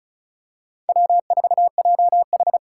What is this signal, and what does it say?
Morse code, heard as a keyed tone, about 36 wpm, 700 Hz. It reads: W4JH
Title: USA Call Signs - 5 Characters - 8x Character Spacing - 35wpm